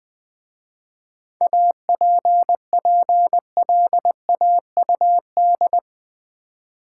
Morse code, heard as a keyed tone, about 20 wpm, 690 Hz. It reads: APPLAUD
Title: 7 Letter Words - First 3 Letters Given - 20wpm